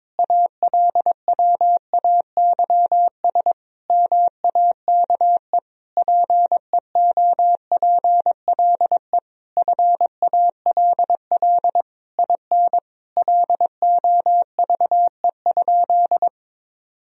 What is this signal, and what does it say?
Morse code, heard as a keyed tone, about 22 wpm, 700 Hz. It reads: ALWAYS MAKE PEOPLE FALL IN LOVE?
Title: Sentences from Top 1000 Words - Review Part 2 - 22wpm